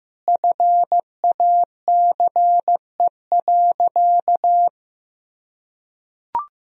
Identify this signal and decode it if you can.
Morse code, heard as a keyed tone, about 16 wpm, 700 Hz. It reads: FACE.
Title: Sentences from Top 1000 Words - Review Part 2 - 15wpm